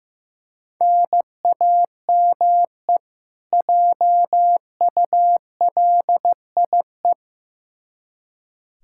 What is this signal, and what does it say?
Morse code, heard as a keyed tone, about 15 wpm, 700 Hz. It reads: NAME JULIE
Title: K1USN SST QSO Practice 15wpm